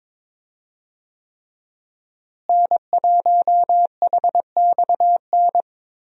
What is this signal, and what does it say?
Morse code, heard as a keyed tone, about 22 wpm, 700 Hz. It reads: N1HXN
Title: USA Call Signs - 5 Characters - Repeated 3x - 22wpm